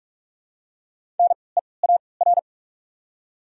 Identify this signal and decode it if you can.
Morse code, heard as a keyed tone, about 47 wpm, spaced as for 26 wpm, 700 Hz. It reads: NEAR
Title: 3x Character Spacing - Top 200 Words - 45wpm